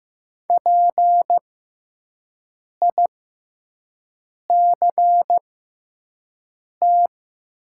Morse code, heard as a keyed tone, about 15 wpm, spaced as for 5 wpm, 700 Hz. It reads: PICT
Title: 6x Character Spacing - Top 300 Words - 15wpm